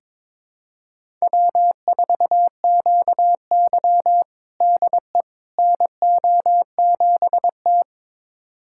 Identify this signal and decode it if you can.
Morse code, heard as a keyed tone, about 21 wpm, 700 Hz. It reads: W4QY DE NO7T
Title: USA Callsign DE Callsign - 4 Characters - 22wpm